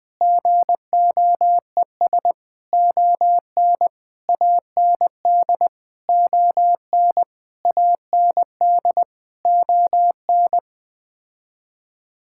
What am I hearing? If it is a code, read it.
Morse code, heard as a keyed tone, about 21 wpm, 700 Hz. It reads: GOES ON AND ON AND ON
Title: Sets of 8 Words - 20wpm